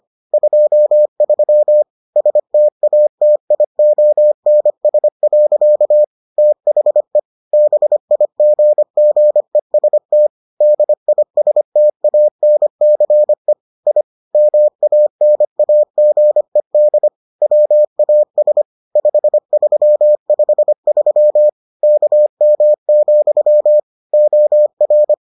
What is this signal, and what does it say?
Morse code, heard as a keyed tone, about 25 wpm, 600 Hz. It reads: 23 STATIONS. THE BIGGEST DISTANCE I MANAGED WAS 5353 KM, OR